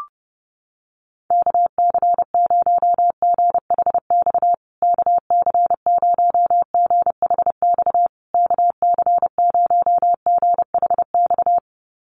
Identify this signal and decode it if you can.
Morse code, heard as a keyed tone, about 30 wpm, 700 Hz. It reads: KC0GHX KC0GHX KC0GHX
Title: USA Call Signs Repeated 3x 30wpm